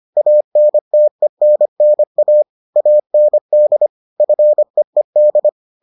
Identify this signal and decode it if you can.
Morse code, heard as a keyed tone, about 25 wpm, 600 Hz. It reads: ANTENNA AND FEED